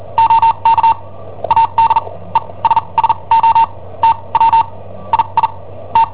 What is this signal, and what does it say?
Morse code, heard as a keyed tone, about 43 wpm, spaced as for 28 wpm, 910 Hz. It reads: OK ADESSO T W II T
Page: Cassetta esercizi preparata da I7OHP per IW5ELP